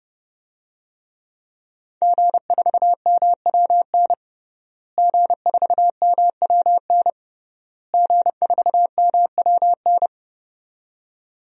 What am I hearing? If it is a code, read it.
Morse code, heard as a keyed tone, about 29 wpm, 700 Hz. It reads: G4MWN G4MWN G4MWN
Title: DXCC-100 Call Signs - 5 Characters - Repeated 3x - 30wpm